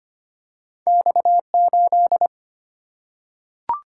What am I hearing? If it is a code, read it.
Morse code, heard as a keyed tone, about 25 wpm, 700 Hz. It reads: X8
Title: Single-Letter Single-Number 25wpm